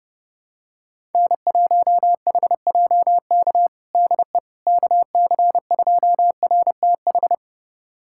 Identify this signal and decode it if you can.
Morse code, heard as a keyed tone, about 32 wpm, 700 Hz. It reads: N1HJK DE KC2RTH
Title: USA Callsign DE Callsign 30wpm